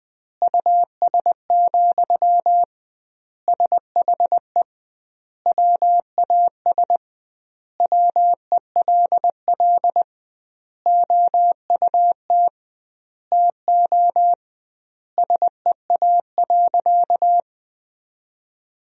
Morse code, heard as a keyed tone, about 21 wpm, 700 Hz. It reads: US, SHE WAS WELL OUT TO SEA.
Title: 2x Word-Spacing - Sentences from Top 400 Words - 20wpm